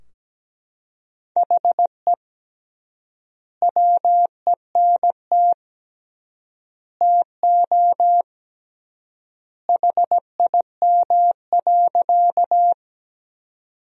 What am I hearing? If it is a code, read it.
Morse code, heard as a keyed tone, about 17 wpm, 700 Hz. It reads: HE WENT TO HIM.
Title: 3x Word-Spacing - Sentences from Top 200 Words - 17wpm